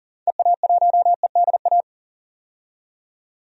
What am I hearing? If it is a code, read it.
Morse code, heard as a keyed tone, about 40 wpm, 700 Hz. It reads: EA1EDA
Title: DXCC-100 Call Signs - 6 Characters - 40wpm